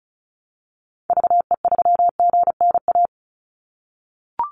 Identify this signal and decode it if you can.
Morse code, heard as a keyed tone, about 34 wpm, 700 Hz. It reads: VE3GNA
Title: Call Signs - 35wpm